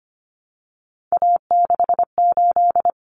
Morse code, heard as a keyed tone, about 25 wpm, 700 Hz. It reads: A68
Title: Common Cut Numbers - 3-Digit 25wpm